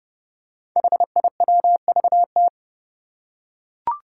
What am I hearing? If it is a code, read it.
Morse code, heard as a keyed tone, about 30 wpm, 700 Hz. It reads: HIWVT
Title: Five Challenging Letter-Number Combinations 30wpm